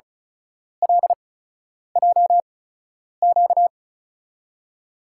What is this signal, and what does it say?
Morse code, heard as a keyed tone, about 34 wpm, spaced as for 10 wpm, 700 Hz. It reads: LJQ
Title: USA Call Signs - 5 Characters - 8x Character Spacing - 35wpm